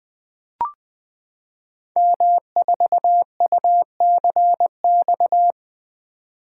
Morse code, heard as a keyed tone, about 20 wpm, 700 Hz. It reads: M4UCX
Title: Code Groups - 5 Letter:Numbers - 20wpm